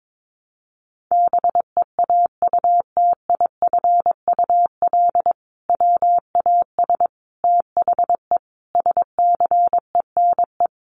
Morse code, heard as a keyed tone, about 22 wpm, 700 Hz. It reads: BEAUTIFUL WAS THE SCENE
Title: Sentences from Top 1100 Words 22wpm